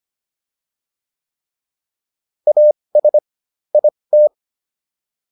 Morse code, heard as a keyed tone, about 25 wpm, spaced as for 20 wpm, 600 Hz. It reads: AS IT